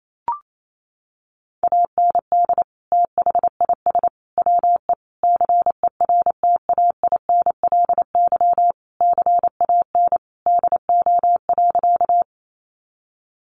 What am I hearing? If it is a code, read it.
Morse code, heard as a keyed tone, about 28 wpm, 700 Hz. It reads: AND THIS WE CERTAINLY CAN DO.
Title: Sentences from Top 900 Words 28wpm